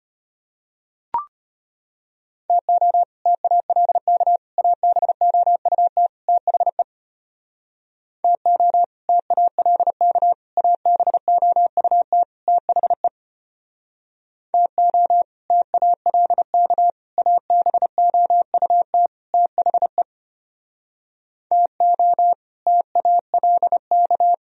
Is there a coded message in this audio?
Yes: Morse code from 2.5 to 24.3 s, heard as a keyed tone, about 32 wpm, 700 Hz. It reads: TO TALK ABOUT THE TO TALK ABOUT THE TO TALK ABOUT THE TO TALK